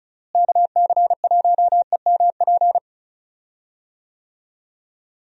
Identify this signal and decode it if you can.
Morse code, heard as a keyed tone, about 36 wpm, 700 Hz. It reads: KC1EMP
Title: USA Call Signs - 6 Characters - Repeated 2x - 35wpm